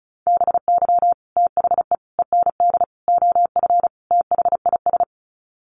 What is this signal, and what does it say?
Morse code, heard as a keyed tone, about 34 wpm, 700 Hz. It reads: BY THE END OF THIS
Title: Sets of 5 Words Encore 35wpm